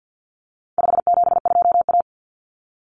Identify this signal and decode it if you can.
Morse code, heard as a keyed tone, about 48 wpm, 700 Hz. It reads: 572A
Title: Common Cut Numbers - 4-Digit 50wpm